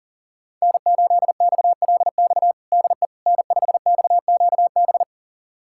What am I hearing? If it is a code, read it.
Morse code, heard as a keyed tone, about 37 wpm, 700 Hz. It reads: N8XLX DE N5XQB
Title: USA Callsign DE Callsign - 5 Characters - 40wpm